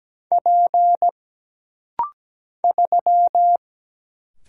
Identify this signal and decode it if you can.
Morse code, heard as a keyed tone, about 17 wpm, 700 Hz. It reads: P 3
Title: Single Letter-Number Alt - Rapid-Fire 17wpm